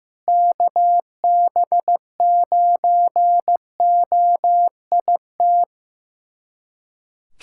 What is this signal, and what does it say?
Morse code, heard as a keyed tone, about 15 wpm, 700 Hz. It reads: KB9OIT
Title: USA Callsign DE Callsign 15wpm